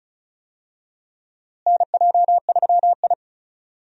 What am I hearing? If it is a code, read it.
Morse code, heard as a keyed tone, about 36 wpm, 700 Hz. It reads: NJ3I
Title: USA Call Signs - 4 Characters - Repeated 2x - 35wpm